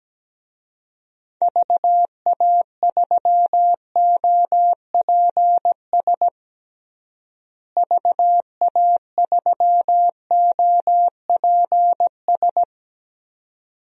Morse code, heard as a keyed tone, about 17 wpm, 700 Hz. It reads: VA3OPS VA3OPS
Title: DXCC-100 Call Signs - 6 Characters - Repeated 2x - 17wpm